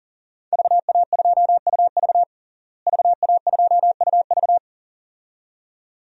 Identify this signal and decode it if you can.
Morse code, heard as a keyed tone, about 37 wpm, 700 Hz. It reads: VA2UV VA2UV
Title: DXCC-100 Call Signs - 5 Characters - Repeated 2x - 40wpm